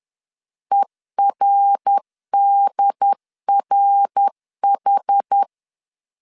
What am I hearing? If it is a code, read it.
Morse code, heard as a keyed tone, about 10 wpm, 800 Hz. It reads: ERDRH
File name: CW / DET / 1 3 1500C S C 10wpm 3335